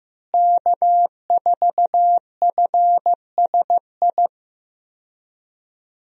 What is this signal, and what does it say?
Morse code, heard as a keyed tone, about 15 wpm, 700 Hz. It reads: K4FSI
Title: USA Call Signs - 5 Characters - Repeated 2x - 15wpm